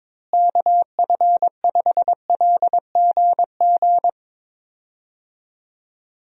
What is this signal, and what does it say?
Morse code, heard as a keyed tone, about 22 wpm, 700 Hz. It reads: KF5LGG
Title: USA Call Signs - 6 Characters - Repeated 2x - 22wpm